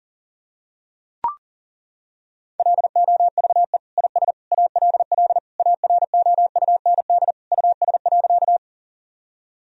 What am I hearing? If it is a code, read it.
Morse code, heard as a keyed tone, about 37 wpm, 700 Hz. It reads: LOVE IS ALL AROUND US.
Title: Sentences from Top 1000 Words - Review Part 3 - 40wpm